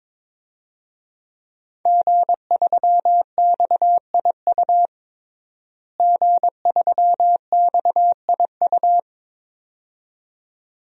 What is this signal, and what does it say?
Morse code, heard as a keyed tone, about 22 wpm, 700 Hz. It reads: G3XIU G3XIU
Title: DXCC-100 Call Signs - 5 Characters - Repeated 2x - 22wpm